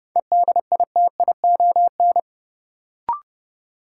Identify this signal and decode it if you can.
Morse code, heard as a keyed tone, about 31 wpm, 700 Hz. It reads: EDITION
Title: Sets of 5 Words - New Word Drill - 30wpm